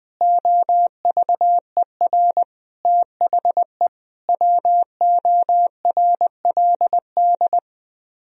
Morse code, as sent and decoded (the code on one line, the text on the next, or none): --- ...- . .-. / - .... / .-- --- .-. .-.. -..
OVER THE WORLD